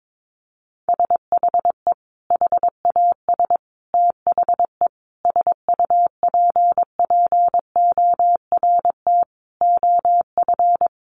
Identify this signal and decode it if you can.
Morse code, heard as a keyed tone, about 22 wpm, 700 Hz. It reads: SHE HAS THE SUPPORT OF
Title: Sentences from Top 800 Words 22wpm